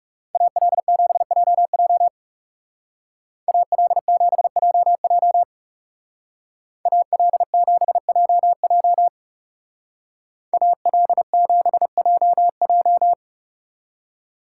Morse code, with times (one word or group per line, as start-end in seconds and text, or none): 0.3-2.1 AL7JJ
3.5-5.4 AL7JJ
6.9-9.1 AL7JJ
10.5-13.1 AL7JJ